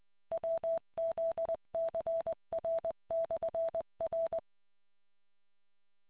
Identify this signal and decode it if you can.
Morse code, heard as a keyed tone, about 20 wpm, 660 Hz. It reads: WZCR/R